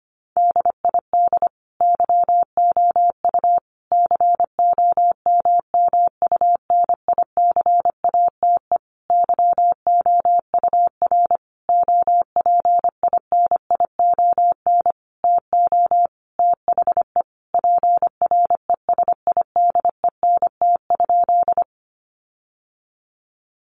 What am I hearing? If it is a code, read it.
Morse code, heard as a keyed tone, about 25 wpm, 700 Hz. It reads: DID YOU COMMUNICATE YOUR OPINION TO THE PRESIDENT?
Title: Sentences from Top 1300 Words 25wpm